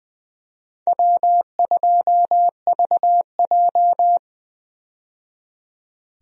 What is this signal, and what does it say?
Morse code, heard as a keyed tone, about 20 wpm, 700 Hz. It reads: W2VJ
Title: USA Call Signs - 4 Characters - 20wpm